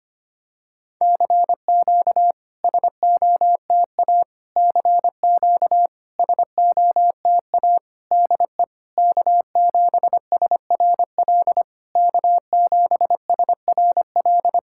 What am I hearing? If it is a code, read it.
Morse code, heard as a keyed tone, about 26 wpm, 700 Hz. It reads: CQ SOTA CQ SOTA DE K7SRL K7SRL
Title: SOTA QSO Practice 25wpm